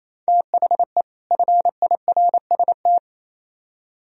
Morse code, heard as a keyed tone, about 36 wpm, 700 Hz. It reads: THE FIRST